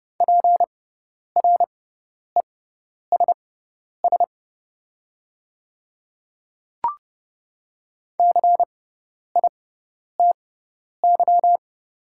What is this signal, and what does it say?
Morse code, heard as a keyed tone, about 30 wpm, spaced as for 10 wpm, 700 Hz. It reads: PRESS CITY
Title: 6x Character Spacing - Top 300 Words - 30wpm